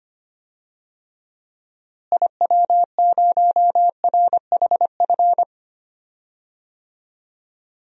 Morse code, heard as a keyed tone, about 25 wpm, 700 Hz. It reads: IW0RHF
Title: DXCC-100 Call Signs - 6 Characters - Repeated 3x - 25wpm